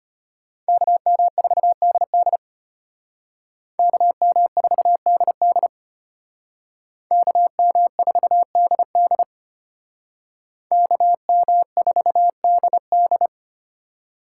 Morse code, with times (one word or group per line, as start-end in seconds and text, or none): 0.7-2.4 KM4DD
3.8-5.7 KM4DD
7.1-9.2 KM4DD
10.7-13.3 KM4DD